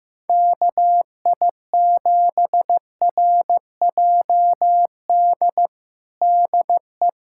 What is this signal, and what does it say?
Morse code, heard as a keyed tone, about 15 wpm, 700 Hz. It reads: KI7RJD DE
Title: USA Callsign DE Callsign - 6 Characters - 15wpm